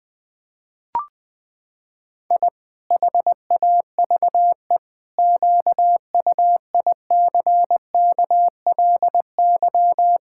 Morse code, heard as a keyed tone, about 21 wpm, 700 Hz. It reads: I HAVE QUICKLY
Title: Pangram Challenge v2 - 20wpm